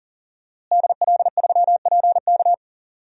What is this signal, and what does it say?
Morse code, heard as a keyed tone, about 40 wpm, 700 Hz. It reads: DL3PK
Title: DXCC-100 Call Signs - 5 Characters - Repeated 2x - 40wpm